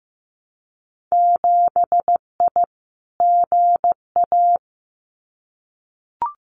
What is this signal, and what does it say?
Morse code, heard as a keyed tone, about 15 wpm, 700 Hz. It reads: 7I GA
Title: Winter Field Day Exchanges Encore 15wpm